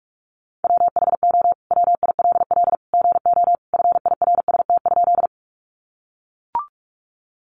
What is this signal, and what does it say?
Morse code, heard as a keyed tone, about 44 wpm, 700 Hz. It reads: WHO WILL GO FIRST?
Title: Sentences from Top 100 Words 45wpm